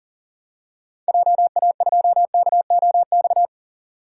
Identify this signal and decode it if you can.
Morse code, heard as a keyed tone, about 40 wpm, 700 Hz. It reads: JA2KOX
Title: DXCC-100 Call Signs - 6 Characters - 40wpm